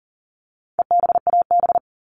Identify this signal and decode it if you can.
Morse code, heard as a keyed tone, about 40 wpm, 700 Hz. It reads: EBAB
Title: All Cut Numbers - 4-Digit 40wpm